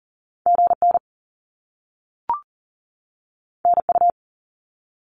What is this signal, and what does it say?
Morse code, heard as a keyed tone, about 40 wpm, 700 Hz. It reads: GN NU
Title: Top Two Letter Combinations in Words 40wpm